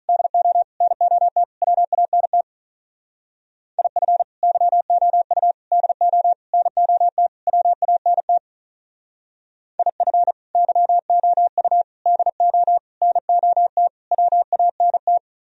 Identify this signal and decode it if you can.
Morse code, heard as a keyed tone, about 44 wpm, 700 Hz. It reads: DO NOT WANT IF YOU DO NOT WANT IF YOU DO NOT WANT